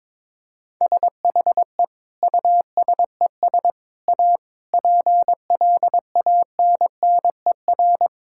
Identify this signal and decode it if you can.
Morse code, heard as a keyed tone, about 22 wpm, 700 Hz. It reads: SHE USES A PLANNER
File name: Difficult Sentences with Hints - 22wpm